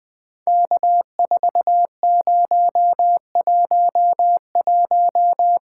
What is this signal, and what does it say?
Morse code, heard as a keyed tone, about 18 wpm, 700 Hz. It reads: K4011
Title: POTA QSO Practice - 20wpm